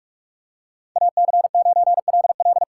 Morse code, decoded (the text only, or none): AC9LL